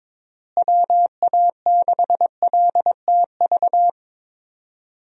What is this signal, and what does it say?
Morse code, heard as a keyed tone, about 22 wpm, 700 Hz. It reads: WA6LTV
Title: USA Call Signs - 6 Characters - Repeated 3x - 22wpm